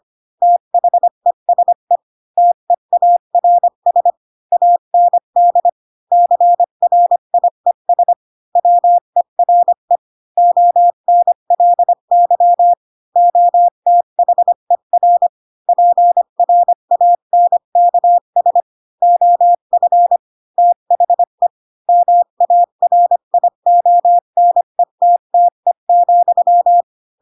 Morse code, heard as a keyed tone, about 25 wpm, 700 Hz. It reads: THESE TEARS AND CRIES WERE ONLY OTHER PRANKS OF THE MARIONETTE,
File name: Pinocchio / Pinocchio ch7 20 25 ik1hge